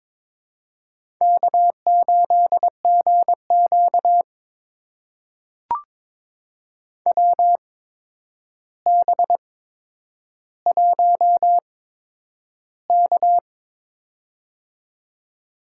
Morse code, heard as a keyed tone, about 22 wpm, spaced as for 9 wpm, 700 Hz. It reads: K8GQ WB1K
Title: USA Call Signs - 4 Characters - 8x Character Spacing - 22wpm